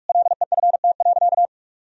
Morse code, heard as a keyed tone, about 46 wpm, 700 Hz. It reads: LEFT.
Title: Sentences from Top 1000 Words - Review Part 2 - 45wpm